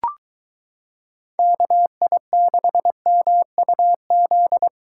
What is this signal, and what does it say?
Morse code, heard as a keyed tone, about 22 wpm, 700 Hz. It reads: KI6MUZ